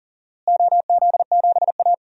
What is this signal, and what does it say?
Morse code, heard as a keyed tone, about 40 wpm, 700 Hz. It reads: OZ7A
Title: DXCC-100 Call Signs - 4 Characters - Repeated 3x - 40wpm